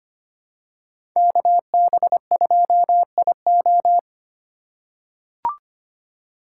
Morse code, heard as a keyed tone, about 25 wpm, 700 Hz. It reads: KB2IO
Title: USA Call Signs - 5 Characters - 2x Character Spacing - 25wpm